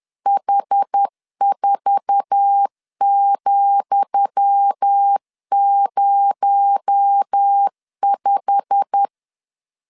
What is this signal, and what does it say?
Morse code, heard as a keyed tone, about 10 wpm, 800 Hz. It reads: H4,05
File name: LEZIONI / 10 SIGNS / 10D COMA 10wpm 3013